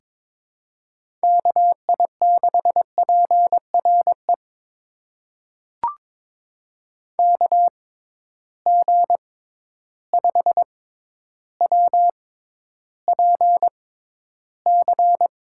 Morse code, heard as a keyed tone, about 22 wpm, spaced as for 11 wpm, 700 Hz. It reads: KI6PRE KG5WPC
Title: USA Call Signs - 6 Characters - 6x Character Spacing - 22wpm